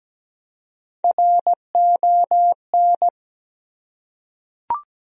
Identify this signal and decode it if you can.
Morse code, heard as a keyed tone, about 17 wpm, 700 Hz. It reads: RON
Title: USA Common Names - Max 4 Letters - 17wpm